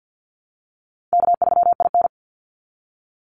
Morse code, heard as a keyed tone, about 50 wpm, 700 Hz. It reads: K3IN
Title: Easy Call Signs - 50wpm